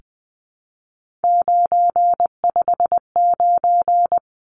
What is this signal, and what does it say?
Morse code, heard as a keyed tone, about 20 wpm, 700 Hz. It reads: 959
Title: Numbers - 3-Digit 20wpm